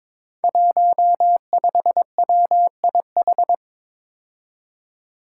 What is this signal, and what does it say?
Morse code, heard as a keyed tone, about 23 wpm, 700 Hz. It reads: 15WIH
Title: Five Challenging Letter-Number Combinations 22wpm